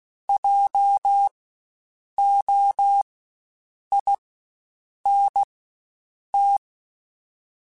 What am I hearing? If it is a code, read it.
Morse code, heard as a keyed tone, about 16 wpm, spaced as for 7 wpm, 800 Hz. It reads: JOINT
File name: Spaced Alphas / Spaced Alphas mp3s / Js sp @15